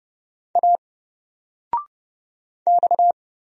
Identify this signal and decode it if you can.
Morse code, heard as a keyed tone, about 30 wpm, 700 Hz. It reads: A X
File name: Single-Letter-Number-30wpm